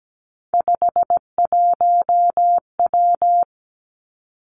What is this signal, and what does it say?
Morse code, heard as a keyed tone, about 17 wpm, 700 Hz. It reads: 51W
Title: Three Challenging Letter-Number Combinations 17wpm